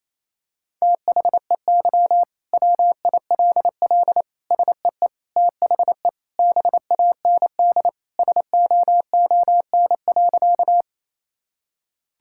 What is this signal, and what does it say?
Morse code, heard as a keyed tone, about 28 wpm, 700 Hz. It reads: THEY WILL SEE THE BAND SOON.
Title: Sentences from Top 1600 Words 28wpm